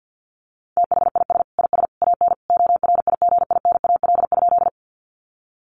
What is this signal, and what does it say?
Morse code, heard as a keyed tone, about 50 wpm, 700 Hz. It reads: THIS IS AN ORIGINAL?
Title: Sentences from Top 900 Words 50wpm